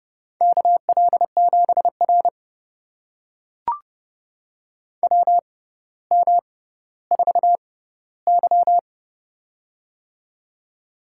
Morse code, heard as a keyed tone, about 30 wpm, spaced as for 15 wpm, 700 Hz. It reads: KL7R WM4Y
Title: USA Call Signs - 4 Characters - 6x Character Spacing - 30wpm